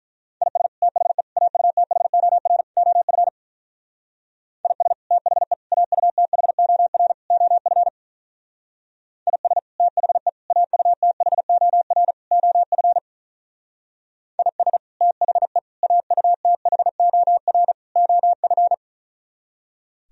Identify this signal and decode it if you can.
Morse code, heard as a keyed tone, about 42 wpm, 700 Hz. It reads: IS THE AUTHOR OF IS THE AUTHOR OF IS THE AUTHOR OF IS THE AUTHOR OF
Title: Speed-Racing - Sets of 4 Words 35wpm